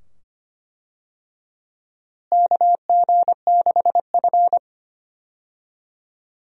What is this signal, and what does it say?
Morse code, heard as a keyed tone, about 25 wpm, 700 Hz. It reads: KG6F
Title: USA Call Signs - 4 Characters - Repeated 2x - 25wpm